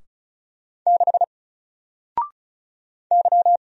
Morse code, heard as a keyed tone, about 36 wpm, 700 Hz. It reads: 6 Y
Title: Single Letter-Number 35wpm